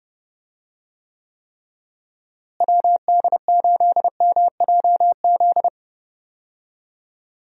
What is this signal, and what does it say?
Morse code, heard as a keyed tone, about 30 wpm, 700 Hz. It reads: WD8MJZ
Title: USA Call Signs - 6 Characters - Repeated 3x - 30wpm